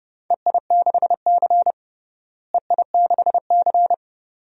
Morse code, heard as a keyed tone, about 30 wpm, 700 Hz. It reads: EI6C EI6C
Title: DXCC-100 Call Signs - 4 Characters - Repeated 2x - 30wpm